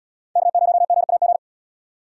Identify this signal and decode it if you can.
Morse code, heard as a keyed tone, about 72 wpm, 700 Hz. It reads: K0GAG